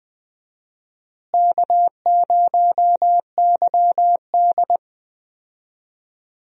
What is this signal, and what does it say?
Morse code, heard as a keyed tone, about 20 wpm, 700 Hz. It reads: K0YD
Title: USA Call Signs - 4 Characters - Repeated 3x - 20wpm